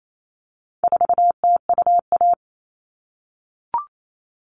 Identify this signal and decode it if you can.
Morse code, heard as a keyed tone, about 28 wpm, 700 Hz. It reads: 4TUA
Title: All Cut Numbers - 4-Digit 28wpm